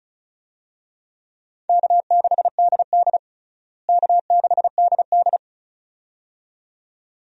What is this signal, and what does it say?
Morse code, heard as a keyed tone, about 34 wpm, 700 Hz. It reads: K6DD K6DD
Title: USA Call Signs - 4 Characters - Repeated 2x - 35wpm